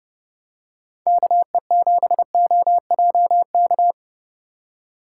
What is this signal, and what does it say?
Morse code, heard as a keyed tone, about 30 wpm, 700 Hz. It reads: KE7OJK